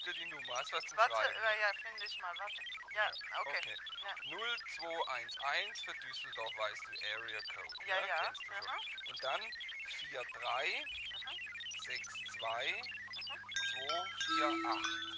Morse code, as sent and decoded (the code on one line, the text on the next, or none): none